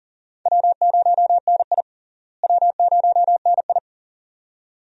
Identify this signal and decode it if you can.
Morse code, heard as a keyed tone, about 40 wpm, 700 Hz. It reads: W0NI W0NI
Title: USA Call Signs - 4 Characters - Repeated 2x - 40wpm